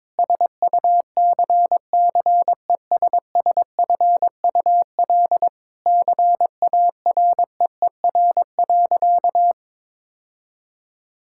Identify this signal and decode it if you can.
Morse code, heard as a keyed tone, about 22 wpm, 700 Hz. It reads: SUCCESSFUL CAREER.
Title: Sentences from Top 1000 Words - Review Part 6 - 22wpm